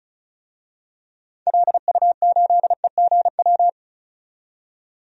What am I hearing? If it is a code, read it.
Morse code, heard as a keyed tone, about 36 wpm, 700 Hz. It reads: LU8EGW
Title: DXCC-100 Call Signs - 6 Characters - 35wpm